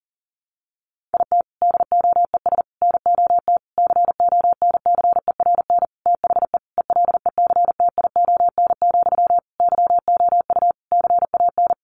Morse code, heard as a keyed tone, about 37 wpm, 700 Hz. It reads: IT DOES NOT CONCERN THE ELECTION, YOU CAN
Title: Sentences from Top 1500 Words 40wpm